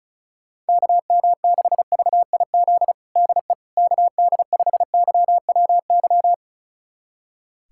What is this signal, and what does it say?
Morse code, heard as a keyed tone, about 36 wpm, 700 Hz. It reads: KM6VIZ DE KD5YWY
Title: USA Callsign DE Callsign - 6 Characters - 35wpm